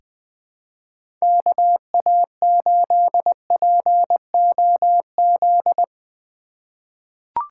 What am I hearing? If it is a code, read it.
Morse code, heard as a keyed tone, about 20 wpm, 700 Hz. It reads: KA8POZ
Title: K1USN SST QSO Practice - 20wpm